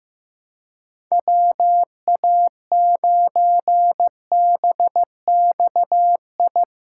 Morse code, heard as a keyed tone, about 15 wpm, 700 Hz. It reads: WA9BXI
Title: Practice QSO - Exchange 4 - Ending - 15wpm